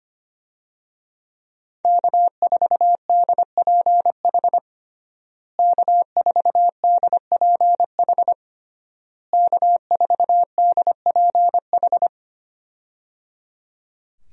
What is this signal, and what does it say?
Morse code, heard as a keyed tone, about 25 wpm, 700 Hz. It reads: K4DPH K4DPH K4DPH
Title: USA Call Signs - 5 Characters - Repeated 3x - 25wpm